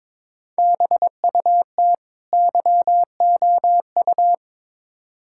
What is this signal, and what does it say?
Morse code, heard as a keyed tone, about 22 wpm, 700 Hz. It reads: BUT YOU
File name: Build A Phrase - 2-Word Phrases - 22wpm